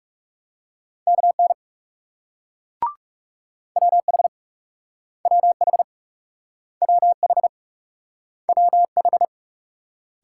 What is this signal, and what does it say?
Morse code, heard as a keyed tone, about 39 wpm, 700 Hz. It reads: KN WH WH WH WH